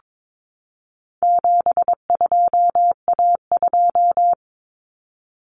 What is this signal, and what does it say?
Morse code, heard as a keyed tone, about 22 wpm, 700 Hz. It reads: 72A2
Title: Common Cut Numbers - 4-Digit 22wpm